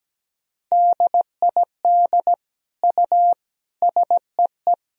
Morse code, heard as a keyed tone, about 17 wpm, 700 Hz. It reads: DID U SEE
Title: Sentences from Top 300 Words and Abbreviations 17wpm